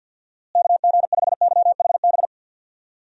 Morse code, heard as a keyed tone, about 51 wpm, 700 Hz. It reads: KG5YHB
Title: USA Call Signs - 6 Characters - Repeated 2x - 50wpm